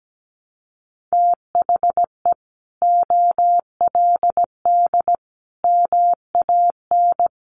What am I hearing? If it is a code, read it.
Morse code, heard as a keyed tone, about 17 wpm, 700 Hz. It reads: THE OLD MAN
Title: Sentences from Top 700 Words 17wpm